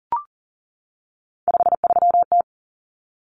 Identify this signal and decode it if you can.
Morse code, heard as a keyed tone, about 40 wpm, 700 Hz. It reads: 53T
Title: Common Cut Numbers - 3-Digit 40wpm